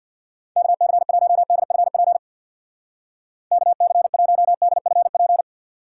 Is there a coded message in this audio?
Yes: Morse code from 0.6 to 5.4 s, heard as a keyed tone, about 63 wpm, 700 Hz. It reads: KC1DFP KC1DFP